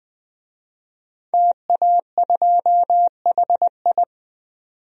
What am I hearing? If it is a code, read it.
Morse code, heard as a keyed tone, about 20 wpm, 700 Hz. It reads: TA2HI
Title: DXCC-100 Call Signs - 5 Characters - 20wpm